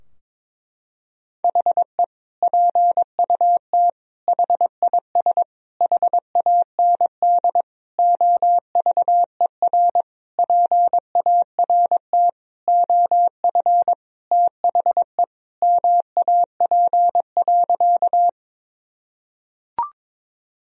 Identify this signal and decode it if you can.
Morse code, heard as a keyed tone, about 21 wpm, 700 Hz. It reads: HE PUT HIS HAND OVER PART OF THE MAP.
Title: Sentences from Top 1000 Words - Review Part 2 - 22wpm